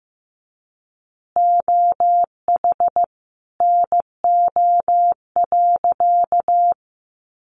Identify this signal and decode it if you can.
Morse code, heard as a keyed tone, about 15 wpm, 700 Hz. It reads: OH NO.
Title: Additional Easy Phrases 15wpm